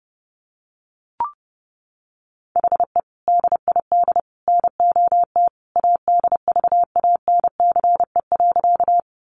Morse code, heard as a keyed tone, about 28 wpm, 700 Hz. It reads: HE DID NOT ADVANCE.
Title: Sentences from Top 1400 Words 30wpm